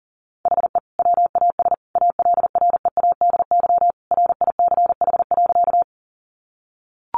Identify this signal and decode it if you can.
Morse code, heard as a keyed tone, about 40 wpm, 700 Hz. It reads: HE WAS ALREADY RICH.
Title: Sentences from Top 1100 Words 40wpm